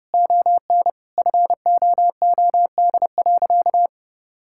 Morse code, heard as a keyed tone, about 30 wpm, 700 Hz. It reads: ON FOOD.
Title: Sentences from Top 1000 Words - Review Part 5 - 30wpm